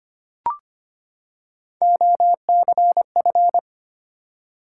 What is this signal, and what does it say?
Morse code, heard as a keyed tone, about 25 wpm, 700 Hz. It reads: OCF
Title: Top 180 QSO Elements 25wpm